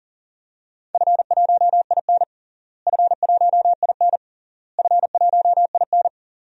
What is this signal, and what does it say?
Morse code, heard as a keyed tone, about 40 wpm, 700 Hz. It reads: F1IN F1IN F1IN
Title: DXCC-100 Call Signs - 4 Characters - Repeated 3x - 40wpm